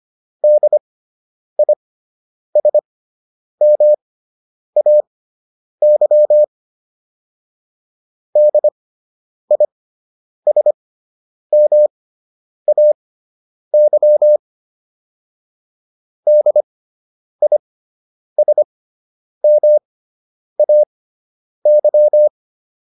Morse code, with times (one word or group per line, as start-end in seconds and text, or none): 0.4-6.4 DISMAY
8.4-14.4 DISMAY
16.3-22.3 DISMAY